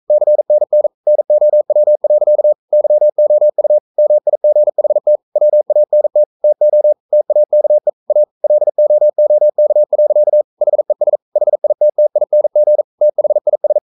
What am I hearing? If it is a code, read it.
Morse code, heard as a keyed tone, about 42 wpm, 600 Hz. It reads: CNN NOW. YOU MIGHT WANT TO TAKE A LOOK. HES HITTING THIS
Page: Romanen Never af Ken Follet på engelsk.